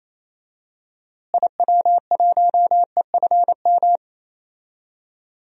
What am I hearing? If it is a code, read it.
Morse code, heard as a keyed tone, about 28 wpm, 700 Hz. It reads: IW1EFM
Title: DXCC-100 Call Signs - 6 Characters - Repeated 2x - 28wpm